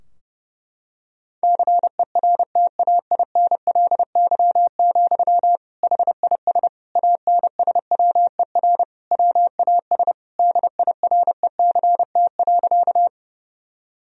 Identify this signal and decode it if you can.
Morse code, heard as a keyed tone, about 30 wpm, 700 Hz. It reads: CERTAINLY, HIS ANSWER WAS DIRECT.
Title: Sentences from Top 1000 Words - Review Part 2 - 30wpm